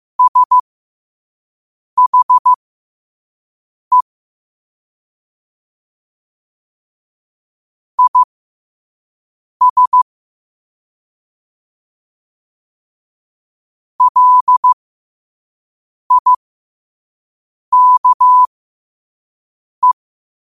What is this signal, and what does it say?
Morse code, heard as a keyed tone, about 15 wpm, spaced as for 5 wpm, 1000 Hz. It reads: SHE IS LIKE
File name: qotdfiles / 5wpm